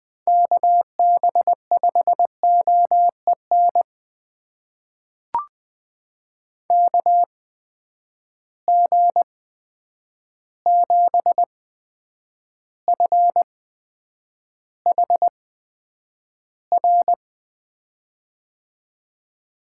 Morse code, heard as a keyed tone, about 20 wpm, spaced as for 9 wpm, 700 Hz. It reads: KB5OEN KG7FHR
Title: USA Call Signs - 8x Character Spacing - 20wpm